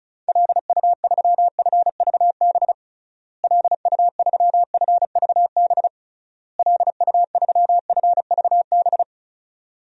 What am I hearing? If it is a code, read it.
Morse code, heard as a keyed tone, about 36 wpm, 700 Hz. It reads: LU3FVB LU3FVB LU3FVB
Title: DXCC-100 Call Signs - 6 Characters - Repeated 3x - 35wpm